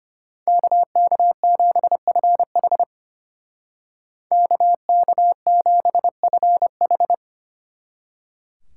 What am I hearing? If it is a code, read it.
Morse code, heard as a keyed tone, about 33 wpm, 700 Hz. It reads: KK7FH KK7FH